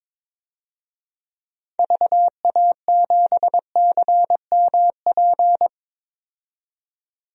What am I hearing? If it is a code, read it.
Morse code, heard as a keyed tone, about 22 wpm, 700 Hz. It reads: VA7CMP
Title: DXCC-100 Call Signs - 6 Characters - 22wpm